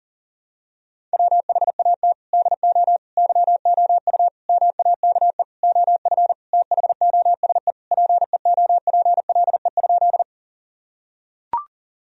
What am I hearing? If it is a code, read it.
Morse code, heard as a keyed tone, about 40 wpm, 700 Hz. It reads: WHAT DO YOU MAKE OF THOSE PEOPLE?
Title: Sentences from Top 1000 Words - Review Part 1 - 40wpm